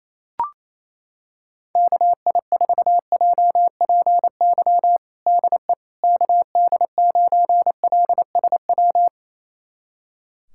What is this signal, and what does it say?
Morse code, heard as a keyed tone, about 28 wpm, 700 Hz. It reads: KI4JPY DE KD9LSW
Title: USA Callsign DE Callsign 28wpm